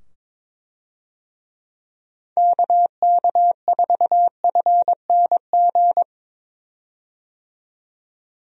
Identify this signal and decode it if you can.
Morse code, heard as a keyed tone, about 22 wpm, 700 Hz. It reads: KK4FNG
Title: USA Call Signs - 6 Characters - 22wpm